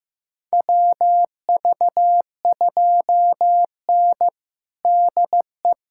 Morse code, heard as a keyed tone, about 15 wpm, 700 Hz. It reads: WV2N DE
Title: USA Callsign DE Callsign - 4 Characters - 15wpm